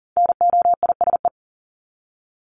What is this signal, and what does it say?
Morse code, heard as a keyed tone, about 40 wpm, 700 Hz. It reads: NOISE
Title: Top 1501-1750 Words 40wpm